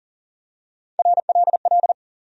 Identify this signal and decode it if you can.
Morse code, heard as a keyed tone, about 37 wpm, 700 Hz. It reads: RLL
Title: USA Call Sign Postfixes - 3 Letters 40wpm